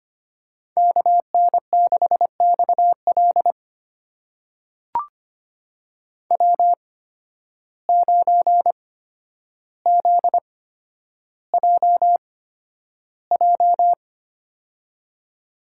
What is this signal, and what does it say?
Morse code, heard as a keyed tone, about 25 wpm, spaced as for 11 wpm, 700 Hz. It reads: KN6XL W9ZJJ
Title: USA Call Signs - 5 Characters - 8x Character Spacing - 25wpm